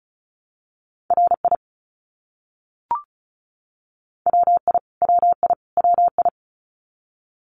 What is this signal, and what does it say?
Morse code, heard as a keyed tone, about 34 wpm, 700 Hz. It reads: RI WI WI WI
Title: US States Repeated 3x 35wpm